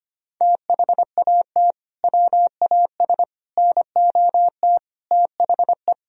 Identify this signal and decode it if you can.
Morse code, heard as a keyed tone, about 25 wpm, 700 Hz. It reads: THAT WAS NOT THE
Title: Sentences from Top 1000 Words - Review Part 4 - 25wpm